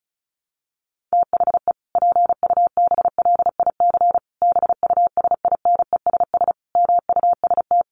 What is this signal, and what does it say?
Morse code, heard as a keyed tone, about 34 wpm, 700 Hz. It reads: THE PUBLIC BUSINESS MUST
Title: Sentences from Top 1300 Words 35wpm